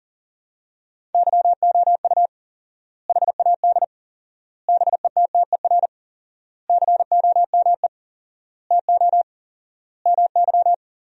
Morse code, heard as a keyed tone, about 40 wpm, 700 Hz. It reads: YOU HAD BETTER COME TO MY
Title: 4x Word-Spacing - Sentences from Top 400 Words - 40wpm